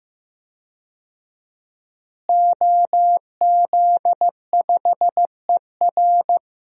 Morse code, heard as a keyed tone, about 15 wpm, 700 Hz. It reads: OZ5ER
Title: DXCC-100 Call Signs - 5 Characters - Repeated 3x - 15wpm